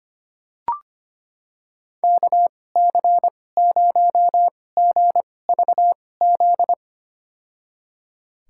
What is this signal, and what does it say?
Morse code, heard as a keyed tone, about 25 wpm, spaced as for 18 wpm, 700 Hz. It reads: KC0GVZ
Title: USA Call Signs - 6 Characters - 2x Character Spacing - 25wpm